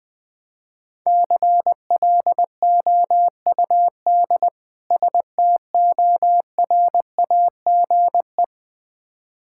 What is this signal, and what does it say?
Morse code, heard as a keyed tone, about 20 wpm, 700 Hz. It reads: CLOUD STORAGE
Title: Difficult Words - Speed-Racing with Hints - 20wpm